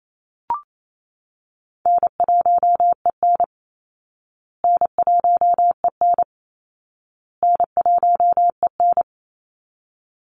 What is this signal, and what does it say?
Morse code, heard as a keyed tone, about 28 wpm, 700 Hz. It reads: N1EN N1EN N1EN
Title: Call Signs Repeated 2x - 28wpm